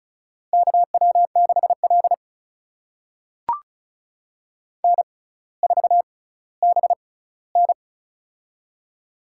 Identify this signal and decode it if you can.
Morse code, heard as a keyed tone, about 36 wpm, spaced as for 18 wpm, 700 Hz. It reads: KW6L N4BN
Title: USA Call Signs - 4 Characters - 6x Character Spacing - 35wpm